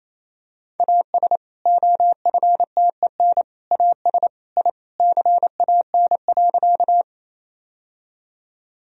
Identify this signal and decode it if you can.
Morse code, heard as a keyed tone, about 28 wpm, 700 Hz. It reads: AS OFTEN AS I CAN.
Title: Sentences from Top 1000 Words - Review Part 1 - 28wpm